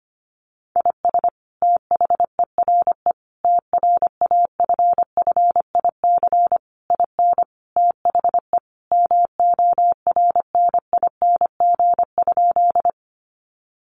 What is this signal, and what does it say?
Morse code, heard as a keyed tone, about 25 wpm, 700 Hz. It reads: IS THERE TRAFFIC IN THE MORNING?
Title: Sentences from Top 1300 Words 25wpm